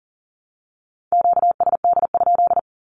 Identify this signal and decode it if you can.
Morse code, heard as a keyed tone, about 37 wpm, 690 Hz. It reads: QSD?
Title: Q-Codes with Spoken Meaning - 40wpm